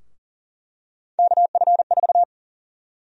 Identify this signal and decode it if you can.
Morse code, heard as a keyed tone, about 40 wpm, 700 Hz. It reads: KF4
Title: USA Call Sign Prefixes - 40wpm